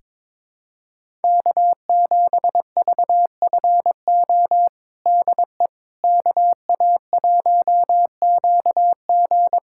Morse code, heard as a keyed tone, about 22 wpm, 700 Hz. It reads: K7VFO DE KA1QG
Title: USA Callsign DE Callsign - 5 Characters - 22wpm